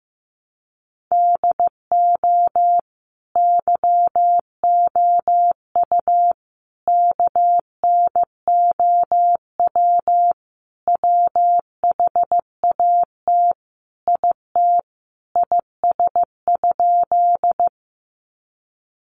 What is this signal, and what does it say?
Morse code, heard as a keyed tone, about 15 wpm, 700 Hz. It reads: DO YOU KNOW WHAT IT IS?
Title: Sentences from Top 100 Words 15wpm